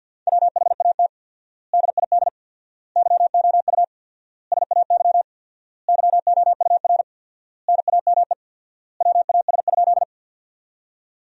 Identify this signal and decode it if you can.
Morse code, heard as a keyed tone, about 53 wpm, 700 Hz. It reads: WHAT DID YOU SAY YOUR NAME WAS?
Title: 4x Word-Spacing - Sentences from Top 200 Words - 50wpm